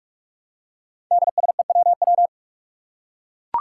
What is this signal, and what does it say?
Morse code, heard as a keyed tone, about 46 wpm, 700 Hz. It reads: DSEWW
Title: Five Challenging Letter-Number Combinations 45wpm